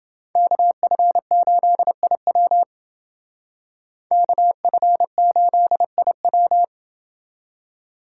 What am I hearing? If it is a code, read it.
Morse code, heard as a keyed tone, about 28 wpm, 700 Hz. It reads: KF8IW KF8IW